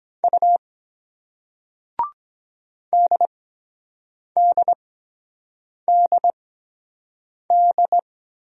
Morse code, heard as a keyed tone, about 23 wpm, spaced as for 7 wpm, 700 Hz. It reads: U DDDD